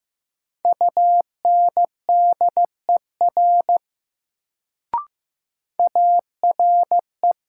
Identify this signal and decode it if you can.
Morse code, heard as a keyed tone, about 15 wpm, 700 Hz. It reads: UNDER ARE
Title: Sets of 2 Words - New Word Drill - 15wpm